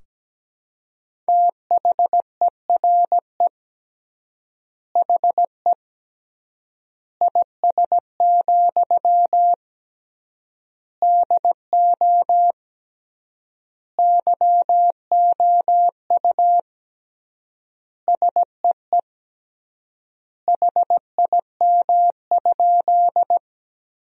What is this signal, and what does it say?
Morse code, heard as a keyed tone, about 17 wpm, 700 Hz. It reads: THERE HE IS, DO YOU SEE HIM?
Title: 3x Word-Spacing - Sentences from Top 100 Words - 17wpm